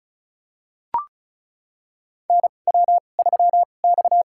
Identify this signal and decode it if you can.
Morse code, heard as a keyed tone, about 34 wpm, spaced as for 25 wpm, 700 Hz. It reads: NW3X
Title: USA Call Signs - 2x Character Spacing - 35wpm